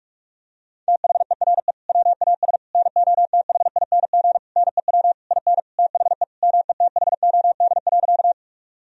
Morse code, heard as a keyed tone, about 41 wpm, 700 Hz. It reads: THERE WAS NOTHING NEW IN THE METHOD.
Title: Sentences from Top 1000 Words - Review Part 3 - 45wpm